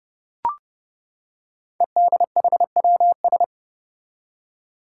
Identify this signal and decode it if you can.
Morse code, heard as a keyed tone, about 30 wpm, 700 Hz. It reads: EDHWS
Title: Five Challenging Letter-Number Combinations 30wpm